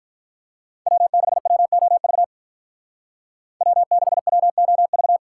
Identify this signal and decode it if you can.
Morse code, heard as a keyed tone, about 44 wpm, 700 Hz. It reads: W6WOV W6WOV